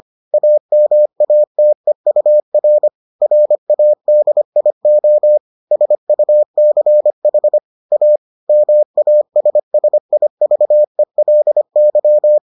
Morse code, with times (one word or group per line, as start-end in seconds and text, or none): 0.3-2.9 AMATEUR
3.2-5.4 RADIO
5.7-7.6 SUCH
7.9-8.2 A
8.5-12.4 MASSIVELY